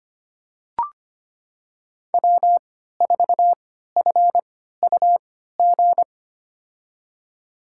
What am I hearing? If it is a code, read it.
Morse code, heard as a keyed tone, about 25 wpm, spaced as for 14 wpm, 700 Hz. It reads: W4FUG
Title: USA Call Signs - 5 Characters - 3x Character Spacing - 25wpm